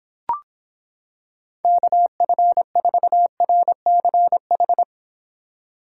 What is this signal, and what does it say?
Morse code, heard as a keyed tone, about 26 wpm, 700 Hz. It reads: KF4RCH